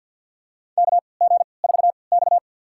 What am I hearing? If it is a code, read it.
Morse code, heard as a keyed tone, about 52 wpm, spaced as for 29 wpm, 700 Hz. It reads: KG4X
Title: USA Call Signs - 4 Characters - 3x Character Spacing - 50wpm